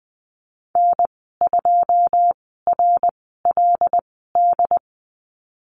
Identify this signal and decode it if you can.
Morse code, heard as a keyed tone, about 20 wpm, spaced as for 14 wpm, 700 Hz. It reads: N2RLD
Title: 245 Lesson 39g - Letter z - Call Signs 20wpm